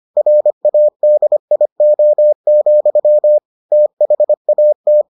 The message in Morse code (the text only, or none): RADIO, THAT